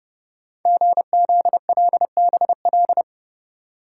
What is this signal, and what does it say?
Morse code, heard as a keyed tone, about 30 wpm, 700 Hz. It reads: GZLBL
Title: Code Groups - 5 Letters - 30wpm